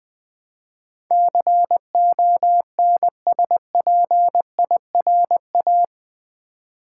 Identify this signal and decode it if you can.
Morse code, heard as a keyed tone, about 20 wpm, 700 Hz. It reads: CONSPIRA
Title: Build A Word - 4 Letters At A Time - 10-Letter Words - 20wpm